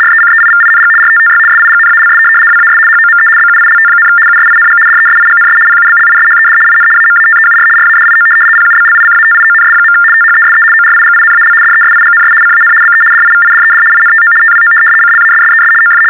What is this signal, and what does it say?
RTTY, about 75 baud. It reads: RECEIVING RTTY SIGNALS WAS HOME BUILT, USING DESIGNS PUBLISHED IN AMATEUR RADIO PUBLICATIONS. THESE ORIGINAL DESIGNS CAN BE DIVIDED INTO TWO CLASSES OF
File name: teletype-RTTY-75